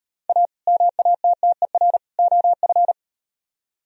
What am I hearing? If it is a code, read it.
Morse code, heard as a keyed tone, about 38 wpm, 700 Hz. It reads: A MATTER OF